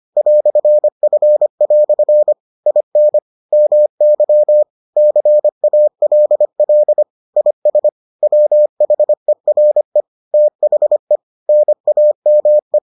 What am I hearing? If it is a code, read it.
Morse code, heard as a keyed tone, about 25 wpm, 600 Hz. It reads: "F" IN MY CALL IS WHERE THE NAME